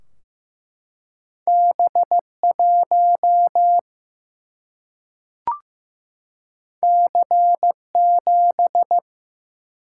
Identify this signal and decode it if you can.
Morse code, heard as a keyed tone, about 15 wpm, 700 Hz. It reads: B 1 C 7
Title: Single-Letter Single-Number 15wpm